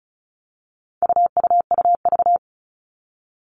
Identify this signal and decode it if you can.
Morse code, heard as a keyed tone, about 34 wpm, 700 Hz. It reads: UUUV
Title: All Cut Numbers - 4-Digit 35wpm